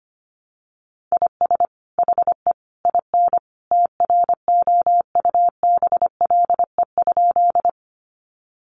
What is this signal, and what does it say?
Morse code, heard as a keyed tone, about 25 wpm, 700 Hz. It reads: IS HE IN TROUBLE?
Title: Sentences from Top 1200 Words 25wpm